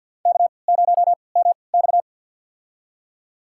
Morse code, heard as a keyed tone, about 51 wpm, spaced as for 28 wpm, 700 Hz. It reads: K0MX
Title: USA Call Signs - 3x Character Spacing - 50wpm